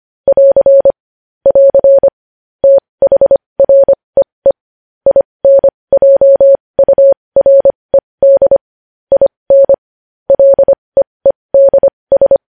Morse code, heard as a keyed tone, about 25 wpm, spaced as for 20 wpm, 550 Hz. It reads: <AR> <AR> THREE INJURED IN LEEDS
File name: podcast / News headlines-1x-25x20